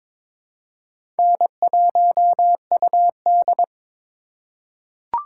Morse code, heard as a keyed tone, about 22 wpm, 700 Hz. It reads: N1UD
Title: USA Call Signs - 4 Characters - 3x Character Spacing - 22wpm